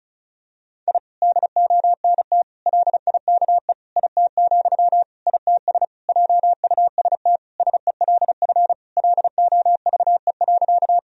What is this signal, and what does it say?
Morse code, heard as a keyed tone, about 36 wpm, 700 Hz. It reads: I DONT LIKE IT, ITS JUST SELF LOVE.
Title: Sentences from Top 1000 Words - Review Part 1 - 35wpm